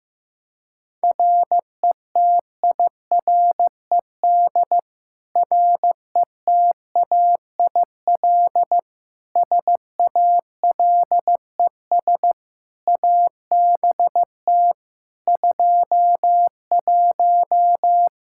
Morse code, heard as a keyed tone, about 16 wpm, 700 Hz. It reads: RETIRED RETAIL SALES ABT 21
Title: Practice QSO - Exchange 3 - More Info - 15wpm